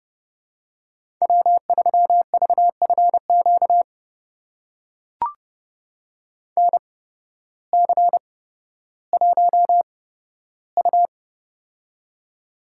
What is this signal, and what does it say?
Morse code, heard as a keyed tone, about 30 wpm, spaced as for 14 wpm, 700 Hz. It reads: W3VFQ NC1U
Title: USA Call Signs - 8x Character Spacing - 30wpm